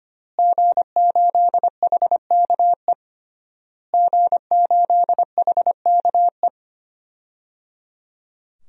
Morse code, heard as a keyed tone, about 25 wpm, 700 Hz. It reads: G8HKE G8HKE
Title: DXCC-100 Call Signs - 5 Characters - Repeated 3x - 25wpm